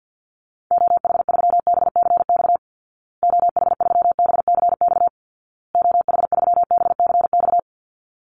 Morse code, heard as a keyed tone, about 50 wpm, 700 Hz. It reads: OH3BCX OH3BCX OH3BCX
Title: Call Signs Repeated 2x - 50wpm